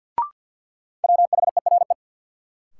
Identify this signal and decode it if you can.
Morse code, heard as a keyed tone, about 52 wpm, 700 Hz. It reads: WHERE
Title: Sentences from Top 200 Words - New Word Drill - 50wpm